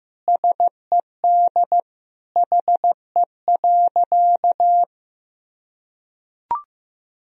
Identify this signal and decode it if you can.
Morse code, heard as a keyed tone, about 15 wpm, 700 Hz. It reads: SED HE.
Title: Sentences from Top 400 Words and Abbreviations 15wpm